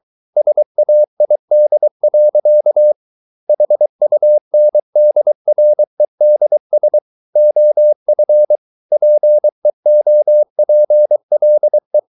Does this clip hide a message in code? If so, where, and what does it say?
Yes: Morse code from 0.4 to 12.0 s, heard as a keyed tone, about 23 wpm, 600 Hz. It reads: SAID. HUNDREDS OF PEOPLE